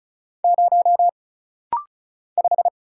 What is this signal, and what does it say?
Morse code, heard as a keyed tone, about 33 wpm, 700 Hz. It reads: 0 5
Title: Single Letter-Number - Rapid-Fire 35wpm